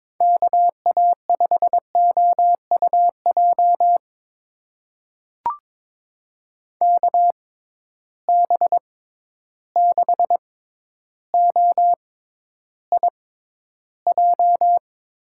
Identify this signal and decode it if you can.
Morse code, heard as a keyed tone, about 22 wpm, spaced as for 11 wpm, 700 Hz. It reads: KA5OUJ KB6OIJ
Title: USA Call Signs - 6 Characters - 6x Character Spacing - 22wpm